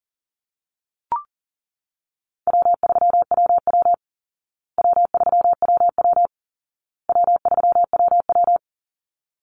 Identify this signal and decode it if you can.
Morse code, heard as a keyed tone, about 40 wpm, 700 Hz. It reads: W3WW W3WW W3WW
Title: Call Signs Repeated 2x - 40wpm